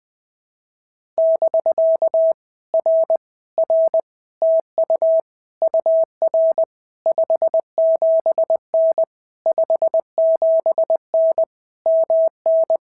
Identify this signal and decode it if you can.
Morse code, heard as a keyed tone, about 20 wpm, 650 Hz. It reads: <BK> R R TU UR 57N 57N MN
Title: POTA QSO Practice - 20wpm